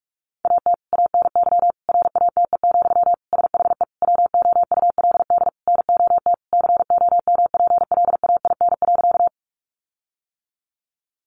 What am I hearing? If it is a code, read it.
Morse code, heard as a keyed tone, about 44 wpm, 700 Hz. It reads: AT ANY RATE, SHE WOULD NOT COMPLAIN.
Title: Sentences from Top 1300 Words 45wpm